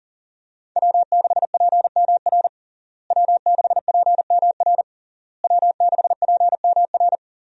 Morse code, heard as a keyed tone, about 37 wpm, 700 Hz. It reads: W6PMR W6PMR W6PMR
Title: USA Call Signs - 5 Characters - Repeated 3x - 40wpm